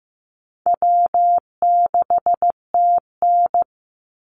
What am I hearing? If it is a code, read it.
Morse code, heard as a keyed tone, about 15 wpm, 700 Hz. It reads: W6TN
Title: Easy Call Signs - 15wpm